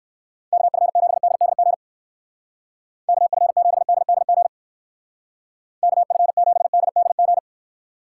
Morse code, heard as a keyed tone, about 59 wpm, 700 Hz. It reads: KF7DDG KF7DDG KF7DDG